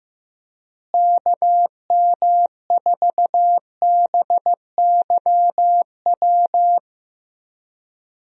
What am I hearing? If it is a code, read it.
Morse code, heard as a keyed tone, about 15 wpm, 700 Hz. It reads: KM4BYW
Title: USA Call Signs - 6 Characters - Repeated 3x - 15wpm